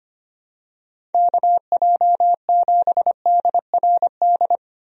Morse code, heard as a keyed tone, about 25 wpm, 700 Hz. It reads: KJ7DRD
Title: USA Call Signs - 6 Characters - Repeated 2x - 25wpm